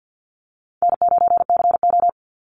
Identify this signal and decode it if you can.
Morse code, heard as a keyed tone, about 50 wpm, 700 Hz. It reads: N9CO
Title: Call Signs - 50wpm